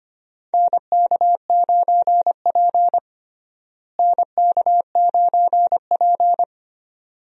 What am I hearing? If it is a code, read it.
Morse code, heard as a keyed tone, about 25 wpm, 700 Hz. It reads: NK9P NK9P
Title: USA Call Signs - 4 Characters - Repeated 2x - 25wpm